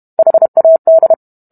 Morse code, heard as a keyed tone, about 32 wpm, 650 Hz. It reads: HAD